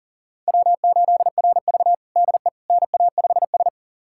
Morse code, heard as a keyed tone, about 40 wpm, 700 Hz. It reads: W8RV DE NA5S
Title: USA Callsign DE Callsign - 4 Characters - 40wpm